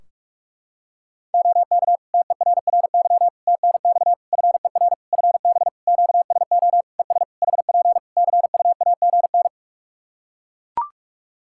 Morse code, heard as a keyed tone, about 41 wpm, 700 Hz. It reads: OK TERRY TNX FER FB QSO ES HP CUAGN
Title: Practice QSO - Exchange 4 - Ending - 45wpm